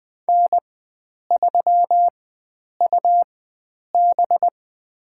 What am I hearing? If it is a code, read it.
Morse code, heard as a keyed tone, about 21 wpm, spaced as for 9 wpm, 700 Hz. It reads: N3UB
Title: USA Call Signs - 4 Characters - 4x Character Spacing - 20wpm